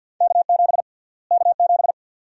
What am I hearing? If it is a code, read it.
Morse code, heard as a keyed tone, about 48 wpm, 700 Hz. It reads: K7 K7
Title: USA Call Sign Prefixes - Repeated 2x - 50wpm